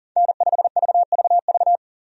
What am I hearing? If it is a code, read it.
Morse code, heard as a keyed tone, about 40 wpm, 700 Hz. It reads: N5VVV
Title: USA Call Signs - 5 Characters - 40wpm